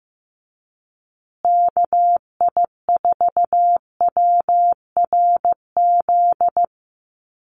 Morse code, heard as a keyed tone, about 15 wpm, 700 Hz. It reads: KI4WRZ
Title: USA Call Signs Repeated 3x 15wpm